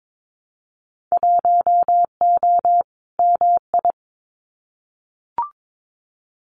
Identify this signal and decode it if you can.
Morse code, heard as a keyed tone, about 22 wpm, 700 Hz. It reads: 1O MI
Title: Winter Field Day Exchanges Encore 22wpm